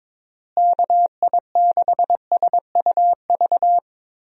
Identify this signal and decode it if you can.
Morse code, heard as a keyed tone, about 22 wpm, 700 Hz. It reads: KI6SUV
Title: USA Call Signs - 6 Characters - Repeated 2x - 22wpm